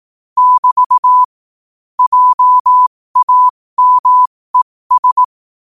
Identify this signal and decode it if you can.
Morse code, heard as a keyed tone, about 18 wpm, spaced as for 15 wpm, 1000 Hz. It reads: <BT> JAMES
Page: 15 WPM morse code quotes